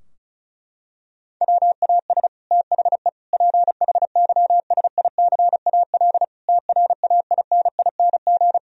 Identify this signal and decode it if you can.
Morse code, heard as a keyed tone, about 34 wpm, 700 Hz. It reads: WAS THE PHYSICAL TRAINING
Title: Sentences from Top 1000 Words - Review Part 4 - 35wpm